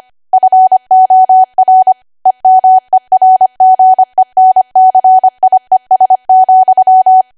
Morse code, heard as a keyed tone, about 25 wpm, 750 Hz. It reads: FOR EMERGENCIES,
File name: Archive / 25 WPM / 25WPM